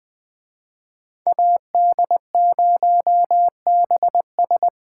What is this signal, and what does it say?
Morse code, heard as a keyed tone, about 21 wpm, 700 Hz. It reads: AD0BS
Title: Speed Racing - USA Call Signs 17wpm